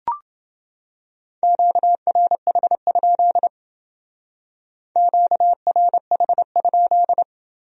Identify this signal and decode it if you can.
Morse code, heard as a keyed tone, about 28 wpm, 700 Hz. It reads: QRH? QRH?